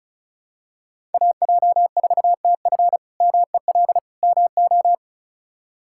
Morse code, heard as a keyed tone, about 36 wpm, 700 Hz. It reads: AJ4TF MEL MO
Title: K1USN SST QSO Practice - 35wpm